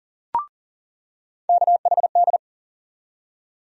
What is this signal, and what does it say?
Morse code, heard as a keyed tone, about 40 wpm, 700 Hz. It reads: KHD
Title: USA Call Sign Postfixes - 3 Letters 40wpm